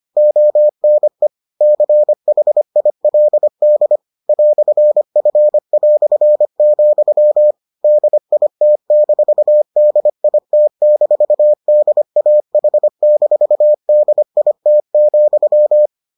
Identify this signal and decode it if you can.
Morse code, heard as a keyed tone, about 25 wpm, 600 Hz. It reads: ONE CHILD "F", DIT-DIT-DAH-DIT,